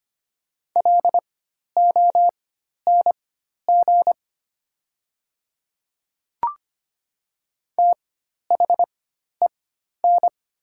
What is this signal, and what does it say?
Morse code, heard as a keyed tone, about 25 wpm, spaced as for 12 wpm, 700 Hz. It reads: LONG THEN
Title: 4x Character Spacing - Top 100 Words - 25wpm